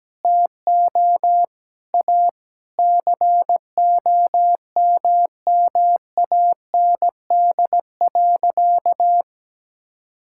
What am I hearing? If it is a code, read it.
Morse code, heard as a keyed tone, about 17 wpm, 700 Hz. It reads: TO A COMMAND.
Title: Sentences from Top 1600 Words 17wpm